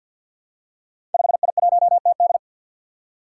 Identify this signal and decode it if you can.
Morse code, heard as a keyed tone, about 46 wpm, 700 Hz. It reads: 5I1TD
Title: Five Challenging Letter-Number Combinations 50wpm